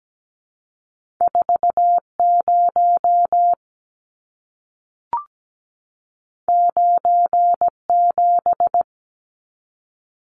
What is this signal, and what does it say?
Morse code, heard as a keyed tone, about 17 wpm, 700 Hz. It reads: 40 97
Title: Numbers - 2-Digit 17wpm